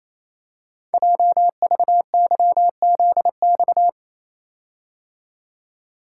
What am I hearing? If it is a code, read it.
Morse code, heard as a keyed tone, about 29 wpm, 700 Hz. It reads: JVYZX
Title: Code Groups - 5 Letters - 28wpm